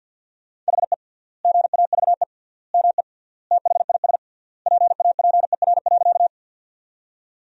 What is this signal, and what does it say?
Morse code, heard as a keyed tone, about 50 wpm, 700 Hz. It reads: HE GAVE ME THIS PAPER.
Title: 3x Word-Spacing - Sentences from Top 500 Words - 50wpm